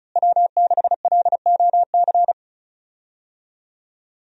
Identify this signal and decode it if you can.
Morse code, heard as a keyed tone, about 34 wpm, 700 Hz. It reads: W6LOC
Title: USA Call Signs - 5 Characters - Repeated 2x - 35wpm